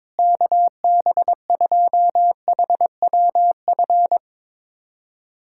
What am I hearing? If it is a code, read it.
Morse code, heard as a keyed tone, about 22 wpm, 700 Hz. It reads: KB2HWF
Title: USA Callsign DE Callsign - 6 Characters - 22wpm